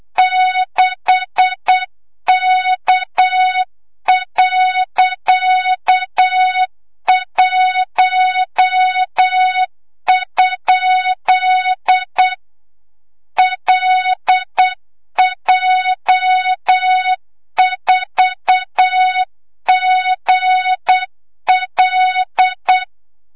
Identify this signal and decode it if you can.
Morse code, heard as a keyed tone, about 8 wpm, 3700 Hz, with noise. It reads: 6K.1? LJ4GL
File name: Morze-HA2MN / hang / 25 GYAKORLAT